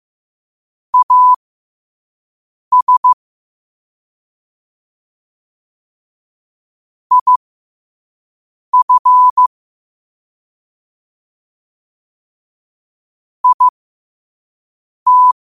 Morse code, heard as a keyed tone, about 15 wpm, spaced as for 5 wpm, 1000 Hz. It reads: AS IF IT
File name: qotdfiles / 5wpm